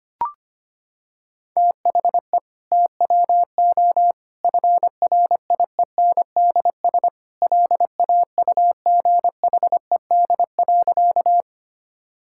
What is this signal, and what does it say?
Morse code, heard as a keyed tone, about 25 wpm, 700 Hz. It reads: THE TWO FRIENDS LAUGHED.
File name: practice / 25wpm / Sentences with 2-Word Phrases - 25wpm